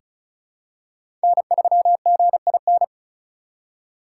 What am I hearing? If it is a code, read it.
Morse code, heard as a keyed tone, about 36 wpm, 700 Hz. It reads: N3GIN
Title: USA Call Signs - 5 Characters - 4x Character Spacing - 35wpm